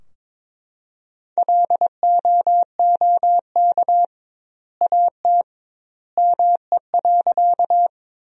Morse code, heard as a keyed tone, about 22 wpm, 700 Hz. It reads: LOOK AT ME.
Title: 2x Word-Spacing - Sentences from Top 200 Words - 22wpm